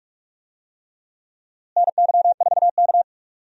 Morse code, heard as a keyed tone, about 46 wpm, 700 Hz. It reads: NY4K
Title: USA Call Signs - 4 Characters - Repeated 3x - 45wpm